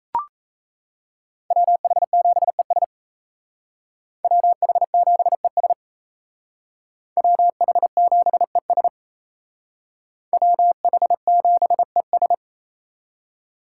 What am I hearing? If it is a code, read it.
Morse code, heard as a keyed tone, about 33 wpm, 700 Hz. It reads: WH7ES WH7ES WH7ES WH7ES
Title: Speed Racing - USA 2x2 Call Signs 28wpm